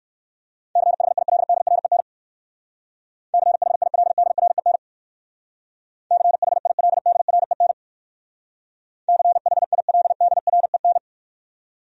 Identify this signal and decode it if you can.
Morse code, heard as a keyed tone, about 58 wpm, 700 Hz. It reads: CHILDREN CHILDREN CHILDREN CHILDREN